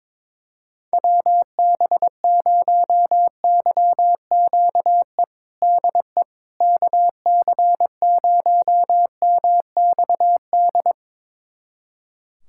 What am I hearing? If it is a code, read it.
Morse code, heard as a keyed tone, about 22 wpm, 700 Hz. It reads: WB0YQE DE KC0MXD
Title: USA Callsign DE Callsign - 6 Characters - 22wpm